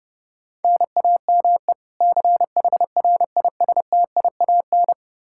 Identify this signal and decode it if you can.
Morse code, heard as a keyed tone, about 32 wpm, 700 Hz. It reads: NAME CHRISTIAN
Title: K1USN SST QSO Practice 30wpm